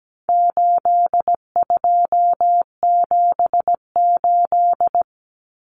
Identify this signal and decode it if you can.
Morse code, heard as a keyed tone, about 17 wpm, 700 Hz. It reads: 8278
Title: Numbers - 4-Digit 17wpm